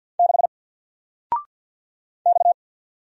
Morse code, heard as a keyed tone, about 48 wpm, 700 Hz. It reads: 6 X
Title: Single Character (Letter:Number:Punctuation:Prosign) 50wpm